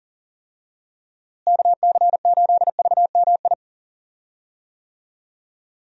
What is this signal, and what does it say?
Morse code, heard as a keyed tone, about 37 wpm, 700 Hz. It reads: KC8VMI
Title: USA Call Signs - 6 Characters - Repeated 3x - 40wpm